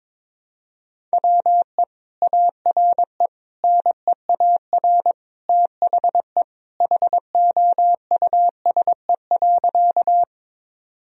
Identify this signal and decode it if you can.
Morse code, heard as a keyed tone, about 22 wpm, 700 Hz. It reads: WE ARE NEAR THE HOUSE.
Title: Sentences from Top 1000 Words - Review Part 1 - 22wpm